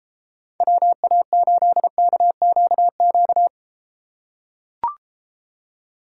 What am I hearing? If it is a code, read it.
Morse code, heard as a keyed tone, about 31 wpm, 700 Hz. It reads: WA8KQQ